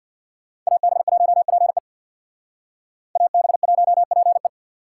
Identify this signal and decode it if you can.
Morse code, heard as a keyed tone, about 55 wpm, 700 Hz. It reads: AB1PE AB1PE
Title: Speed Racing - USA 2x2 Call Signs 50wpm